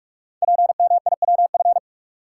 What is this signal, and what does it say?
Morse code, heard as a keyed tone, about 44 wpm, 700 Hz. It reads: PMIWF
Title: Code Groups - 5 Letter:Numbers - 45wpm